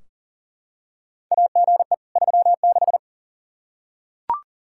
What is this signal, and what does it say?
Morse code, heard as a keyed tone, about 40 wpm, 700 Hz. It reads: AGE 36
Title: Practice QSO - Exchange 3 - More Info - 40wpm